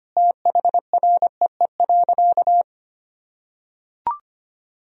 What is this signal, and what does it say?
Morse code, heard as a keyed tone, about 25 wpm, 700 Hz. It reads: THREE.
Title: Sentences from Top 1000 Words - Review Part 2 - 25wpm